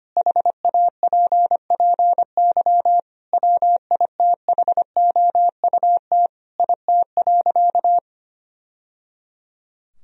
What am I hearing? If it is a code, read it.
Morse code, heard as a keyed tone, about 25 wpm, 700 Hz. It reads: HAPPY WITHOUT IT.
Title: Sentences from Top 1000 Words - Review Part 3 - 25wpm